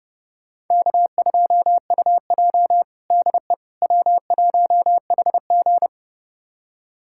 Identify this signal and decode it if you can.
Morse code, heard as a keyed tone, about 30 wpm, 700 Hz. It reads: K2UJ DE W1HG
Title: USA Callsign DE Callsign - 4 Characters - 30wpm